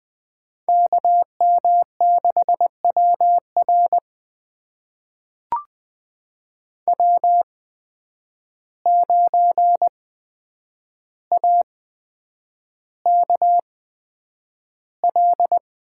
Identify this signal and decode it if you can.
Morse code, heard as a keyed tone, about 20 wpm, spaced as for 9 wpm, 700 Hz. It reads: KM6WR W9AKL
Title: USA Call Signs - 5 Characters - 8x Character Spacing - 20wpm